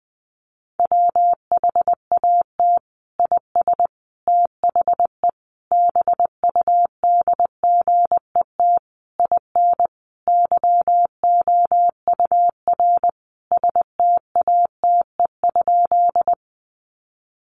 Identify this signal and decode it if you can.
Morse code, heard as a keyed tone, about 20 wpm, 700 Hz. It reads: WHAT IS THE BUDGET IN YOUR STATE?
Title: Sentences from Top 1200 Words 20wpm